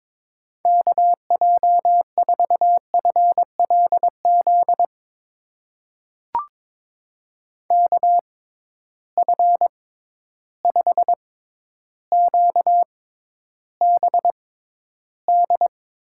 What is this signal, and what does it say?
Morse code, heard as a keyed tone, about 22 wpm, spaced as for 11 wpm, 700 Hz. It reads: KJ4FLZ KF5QBD
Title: USA Call Signs - 6x Character Spacing - 22wpm